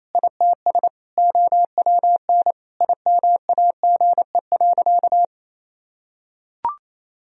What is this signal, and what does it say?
Morse code, heard as a keyed tone, about 28 wpm, 700 Hz. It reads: ITS OWN IMAGE.
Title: Sentences from Top 1000 Words - Review Part 4 - 28wpm